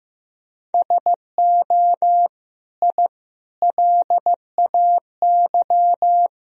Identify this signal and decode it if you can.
Morse code, heard as a keyed tone, about 15 wpm, 700 Hz. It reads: SO I LAY
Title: Sentences from Top 1000 Words - Review Part 2 - 15wpm